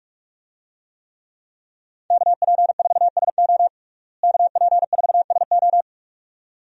Morse code, heard as a keyed tone, about 48 wpm, 700 Hz. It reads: KP4SO KP4SO
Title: DXCC-100 Call Signs - 5 Characters - Repeated 2x - 45wpm